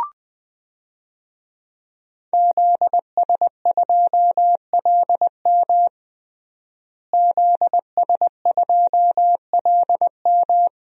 Morse code, heard as a keyed tone, about 20 wpm, 700 Hz. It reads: ZS2LM ZS2LM
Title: DXCC-100 Call Signs - 5 Characters - Repeated 3x - 20wpm